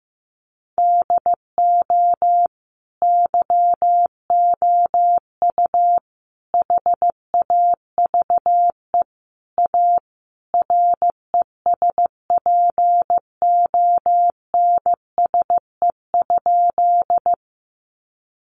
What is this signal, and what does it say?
Morse code, heard as a keyed tone, about 15 wpm, 700 Hz. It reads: DO YOU HAVE A RESPONSE?
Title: Sentences from Top 1400 Words 15wpm